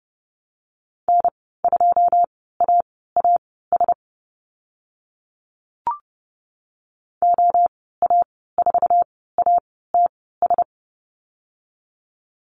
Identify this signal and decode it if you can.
Morse code, heard as a keyed tone, about 30 wpm, spaced as for 17 wpm, 700 Hz. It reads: N2AAS OA4ATS
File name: Lesson 15g - Number 5 - Call Signs 30wpm